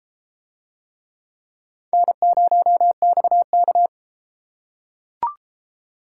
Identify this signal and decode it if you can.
Morse code, heard as a keyed tone, about 32 wpm, 700 Hz. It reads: N0XK